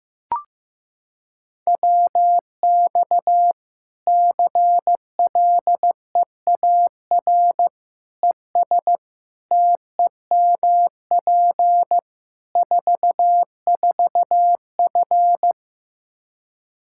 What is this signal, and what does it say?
Morse code, heard as a keyed tone, about 15 wpm, 700 Hz. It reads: WX CLEAR ES TEMP 44F
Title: Practice QSO - Exchange 2 - Info - 15wpm